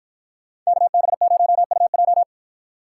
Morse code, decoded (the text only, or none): KB0UJ